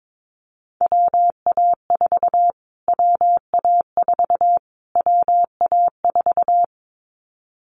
Morse code, heard as a keyed tone, about 22 wpm, 700 Hz. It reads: WA4 WA4 WA4
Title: Callsign Prefixes for Top 25 DXCC - Repeated 3x - 22wpm